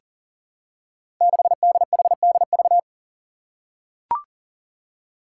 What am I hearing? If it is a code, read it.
Morse code, heard as a keyed tone, about 39 wpm, 700 Hz. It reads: 6DHDV
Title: Five Challenging Letter-Number Combinations 40wpm